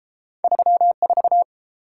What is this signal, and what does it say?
Morse code, heard as a keyed tone, about 29 wpm, 700 Hz. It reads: 34